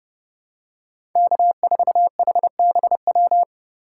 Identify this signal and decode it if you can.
Morse code, heard as a keyed tone, about 30 wpm, 700 Hz. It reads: K4HBW
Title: USA Call Signs - 5 Characters - 4x Character Spacing - 30wpm